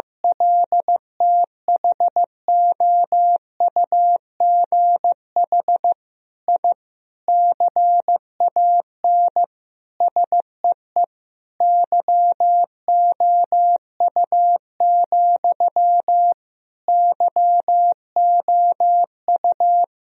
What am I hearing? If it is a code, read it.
Morse code, heard as a keyed tone, about 16 wpm, 700 Hz. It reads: LTHOUGH I CAN SEE YOU, YOU
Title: Sentences from Top 1000 Words - Review Part 3 - 15wpm